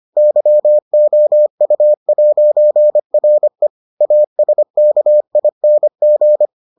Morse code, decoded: YOU'RE ASKING

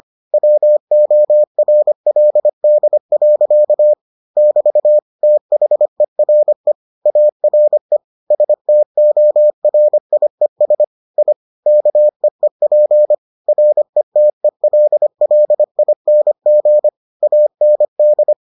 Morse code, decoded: WORLD. <BT> THERE ARE STORIES I KEEP RETELLING AND

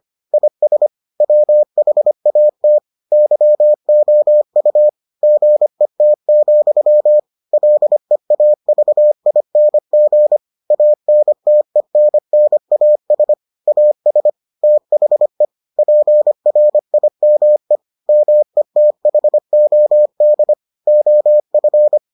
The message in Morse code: IS WHAT YOU GET, LEAVING ANTENNAS AS THE PRIME METHOD OF

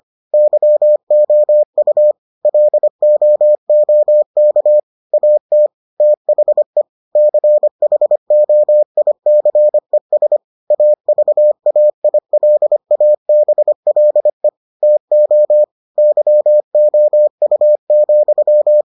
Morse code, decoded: YOU LOOK AT THE CHOICES AVAILABLE TO YOU,

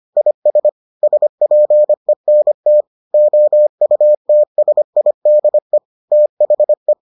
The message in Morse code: IS SPENT OUTSIDE THE

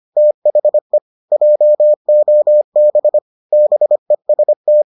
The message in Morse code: THE JOB BEST

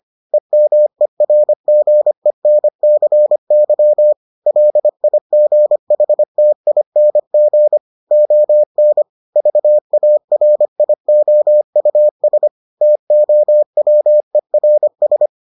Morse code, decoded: EMERGENCY LIGHTING ON VARIOUS TOWERS

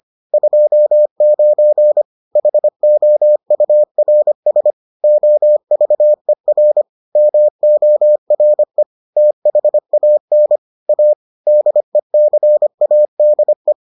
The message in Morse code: 29 HOURS OVER MORE THAN A DECADE

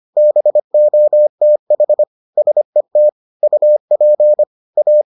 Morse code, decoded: BOTH SET UP A

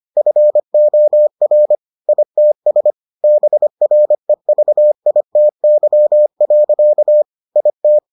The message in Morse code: FOR ITS BREVITY. IT

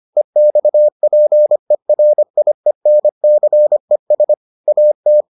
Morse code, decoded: EXPERIENCES AT

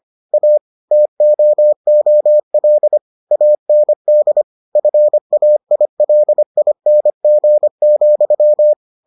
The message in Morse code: A TOOL AND FAILING,

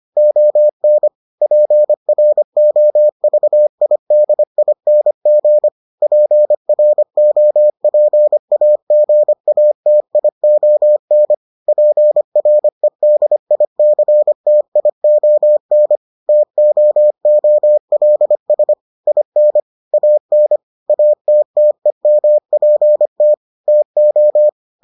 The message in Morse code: ON PROVIDING PROPAGATION PREDICTION TOOLS IN AN ATTEMPT TO

